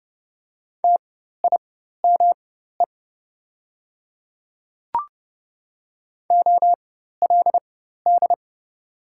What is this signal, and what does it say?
Morse code, heard as a keyed tone, about 30 wpm, spaced as for 14 wpm, 700 Hz. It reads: TIME OLD
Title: 4x Character Spacing - Top 200 Words - 30wpm